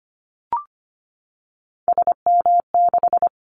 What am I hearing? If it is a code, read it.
Morse code, heard as a keyed tone, about 25 wpm, 700 Hz. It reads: SM6
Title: Callsign Prefixes for Top 25 DXCC - Repeated 3x - 25wpm